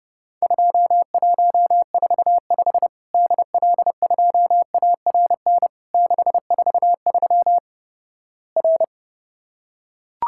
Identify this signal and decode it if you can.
Morse code, heard as a keyed tone, about 30 wpm, 700 Hz. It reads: 2145 DL2ARN 643 R
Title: Worked All Europe DX Contest - QTC Exchange - 30wpm